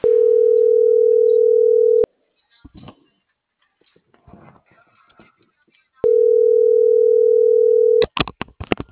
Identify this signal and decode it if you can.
Ringback tone (North America), 440 + 480 Hz.